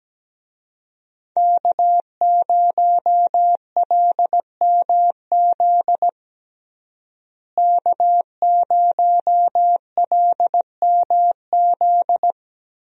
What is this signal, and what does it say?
Morse code, heard as a keyed tone, about 17 wpm, 700 Hz. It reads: K0LMZ K0LMZ
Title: USA Call Signs - 5 Characters - Repeated 3x - 17wpm